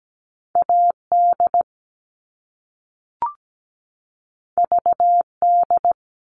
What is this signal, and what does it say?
Morse code, heard as a keyed tone, about 17 wpm, 700 Hz. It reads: AD VD
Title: All Cut Numbers - 2-Digit 17wpm